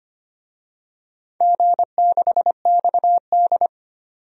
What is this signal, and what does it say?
Morse code, heard as a keyed tone, about 25 wpm, 700 Hz. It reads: G6XD
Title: DXCC-100 Call Signs - 4 Characters - 25wpm